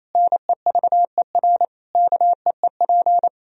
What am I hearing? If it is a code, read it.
Morse code, heard as a keyed tone, about 28 wpm, 700 Hz. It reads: NEVER KEEP
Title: Sentences from Top 1000 Words - Review Part 4 - 28wpm